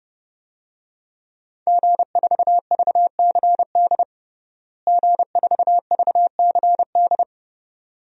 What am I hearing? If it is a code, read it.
Morse code, heard as a keyed tone, about 30 wpm, 700 Hz. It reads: G4VCD G4VCD
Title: DXCC-100 Call Signs - 5 Characters - Repeated 2x - 30wpm